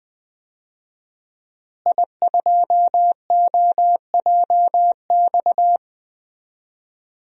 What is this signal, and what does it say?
Morse code, heard as a keyed tone, about 20 wpm, 700 Hz. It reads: I2OJX
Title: DXCC-100 Call Signs - 5 Characters - Repeated 3x - 20wpm